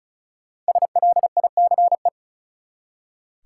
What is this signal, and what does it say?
Morse code, heard as a keyed tone, about 36 wpm, 700 Hz. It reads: SLICE
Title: Five Letter Words - 35wpm